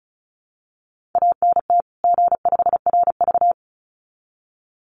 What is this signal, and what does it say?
Morse code, heard as a keyed tone, about 34 wpm, 700 Hz. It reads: ANT G5RV
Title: Common QSO Phrases 35wpm